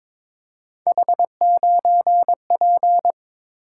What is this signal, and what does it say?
Morse code, heard as a keyed tone, about 22 wpm, 700 Hz. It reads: H9P
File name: practice / 22wpm / Three Letter-Number Combinations - 22wpm